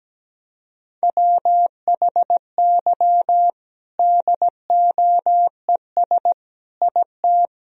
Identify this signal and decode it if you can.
Morse code, heard as a keyed tone, about 17 wpm, 700 Hz. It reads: WHY DOES IT
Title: Sentences from Top 1000 Words - Review Part 3 - 17wpm